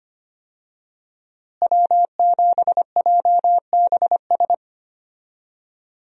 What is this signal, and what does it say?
Morse code, heard as a keyed tone, about 25 wpm, 700 Hz. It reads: W7JBS
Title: USA Call Signs - 5 Characters - Repeated 2x - 25wpm